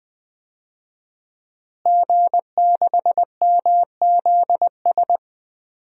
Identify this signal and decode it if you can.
Morse code, heard as a keyed tone, about 20 wpm, 700 Hz. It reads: G6MZS
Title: DXCC-100 Call Signs - 5 Characters - Repeated 2x - 20wpm